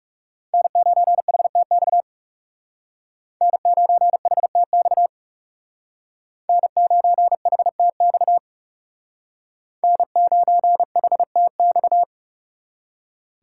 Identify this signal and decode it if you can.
Morse code, heard as a keyed tone, about 40 wpm, 700 Hz. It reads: N9HTX N9HTX N9HTX N9HTX